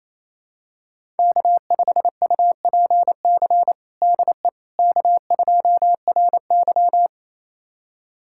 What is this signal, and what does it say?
Morse code, heard as a keyed tone, about 28 wpm, 700 Hz. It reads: K5UPC DE K2RY
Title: USA Callsign DE Callsign 28wpm